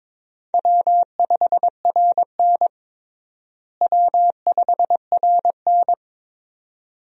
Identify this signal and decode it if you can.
Morse code, heard as a keyed tone, about 22 wpm, 700 Hz. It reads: W5RN W5RN
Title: USA Call Signs - 4 Characters - Repeated 2x - 22wpm